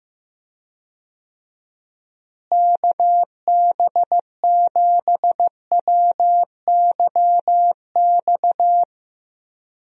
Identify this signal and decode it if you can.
Morse code, heard as a keyed tone, about 15 wpm, 700 Hz. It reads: KB7WYX
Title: USA Call Signs - 6 Characters - Repeated 2x - 15wpm